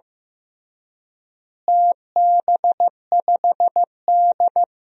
Morse code, heard as a keyed tone, about 16 wpm, 700 Hz. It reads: TB5D
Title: DXCC-100 Call Signs - 4 Characters - Repeated 3x - 15wpm